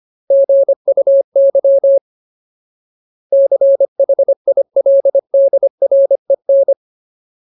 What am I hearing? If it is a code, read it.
Morse code, heard as a keyed tone, about 25 wpm, 550 Hz. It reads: GUY CHILDREN
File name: Ultimate Word List / 25WPM Ultimate